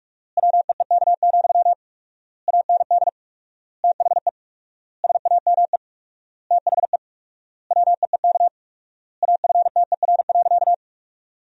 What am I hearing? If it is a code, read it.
Morse code, heard as a keyed tone, about 41 wpm, 700 Hz. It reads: WEEK, AND THE SAME THE WEEK AFTER.
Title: 4x Word-Spacing - Sentences from Top 500 Words - 45wpm